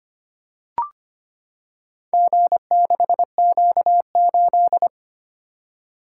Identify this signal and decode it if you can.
Morse code, heard as a keyed tone, about 25 wpm, 700 Hz. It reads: G6Q8
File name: practice / 25wpm / Four Letter-Number Combinations - 25wpm